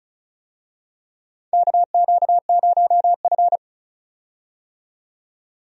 Morse code, heard as a keyed tone, about 36 wpm, 700 Hz. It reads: KQ0F
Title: USA Call Signs - 4 Characters - 35wpm